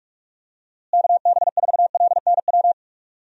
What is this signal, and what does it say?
Morse code, heard as a keyed tone, about 46 wpm, 700 Hz. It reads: KB4LNW
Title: USA Call Signs - 6 Characters - Repeated 2x - 45wpm